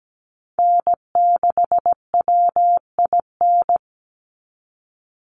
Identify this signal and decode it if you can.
Morse code, heard as a keyed tone, about 17 wpm, 700 Hz. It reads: N6WIN
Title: Call Signs Repeated 2x - 17wpm